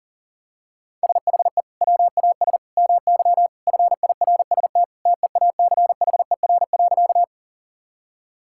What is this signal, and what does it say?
Morse code, heard as a keyed tone, about 40 wpm, 700 Hz. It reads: SHE WAS MY FIRST TEACHER.
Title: Sentences from Top 1000 Words - Review Part 6 - 40wpm